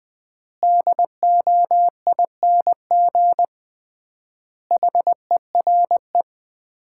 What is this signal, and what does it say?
Morse code, heard as a keyed tone, about 20 wpm, 700 Hz. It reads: DOING HERE
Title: 3x Word-Spacing - Sets of 5 Words - 20wpm